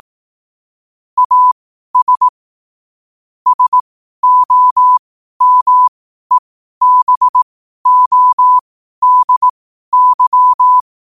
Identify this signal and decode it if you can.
Morse code, heard as a keyed tone, about 18 wpm, spaced as for 12 wpm, 1000 Hz. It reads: AS SOMEBODY
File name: 12wpm